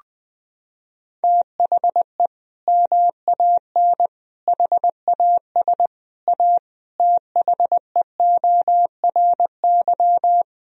Morse code, heard as a keyed tone, about 20 wpm, 700 Hz. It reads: THE MAN HAS A THEORY
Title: Sentences from Top 1000 Words - Review Part 3 - 20wpm